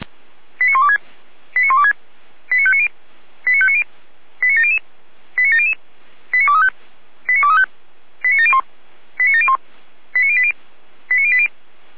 Five-tone selective call (ZVEI), heard as quick runs of five tones, tones of about 80 ms. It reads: 89126 89126 89590 89590 89700 89700 89235 89235 89701 89701 89080 89080